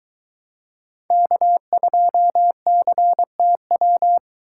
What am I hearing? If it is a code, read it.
Morse code, heard as a keyed tone, about 22 wpm, 700 Hz. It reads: K2CTW